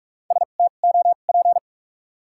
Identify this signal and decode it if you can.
Morse code, heard as a keyed tone, about 46 wpm, spaced as for 33 wpm, 700 Hz. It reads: STOP
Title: 2x Character Spacing - Top 300 Words - 45wpm